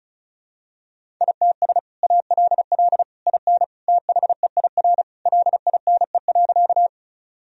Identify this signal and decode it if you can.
Morse code, heard as a keyed tone, about 36 wpm, 700 Hz. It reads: ITS ALL IN THEIR LINE.
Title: Sentences from Top 1000 Words - Review Part 1 - 35wpm